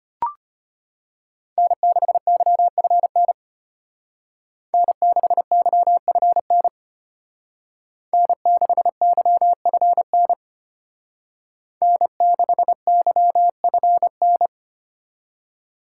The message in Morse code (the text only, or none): N6YFN N6YFN N6YFN N6YFN